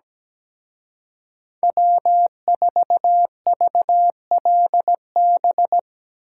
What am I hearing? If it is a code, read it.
Morse code, heard as a keyed tone, about 19 wpm, 700 Hz. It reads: W4VLB
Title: Speed Racing - USA 1x3 Call Signs 17wpm